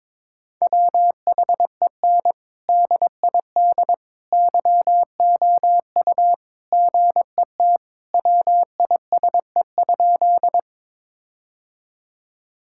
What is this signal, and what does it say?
Morse code, heard as a keyed tone, about 22 wpm, 700 Hz. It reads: WHEN DID YOU GET WISE?
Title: Sentences from Top 1600 Words 22wpm